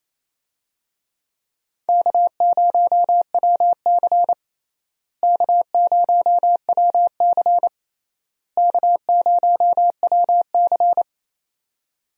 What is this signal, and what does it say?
Morse code, heard as a keyed tone, about 28 wpm, 700 Hz. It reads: K0WC K0WC K0WC
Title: USA Call Signs - 4 Characters - Repeated 3x - 28wpm